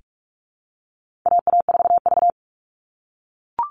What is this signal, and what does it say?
Morse code, heard as a keyed tone, about 44 wpm, 700 Hz. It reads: AA4V
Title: All Cut Numbers - 4-Digit 45wpm